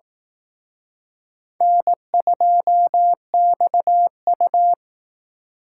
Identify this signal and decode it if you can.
Morse code, heard as a keyed tone, about 18 wpm, 700 Hz. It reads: N2XU
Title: Speed Racing - USA 1x2 Call Signs 15wpm